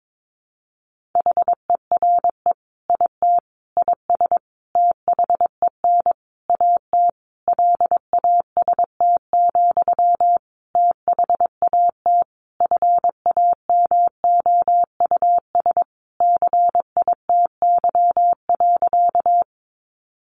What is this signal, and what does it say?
Morse code, heard as a keyed tone, about 22 wpm, 700 Hz. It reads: HERE IT IS THEN AT LAST, THAT FAMOUS CITY.
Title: Sentences from Top 1500 Words 22wpm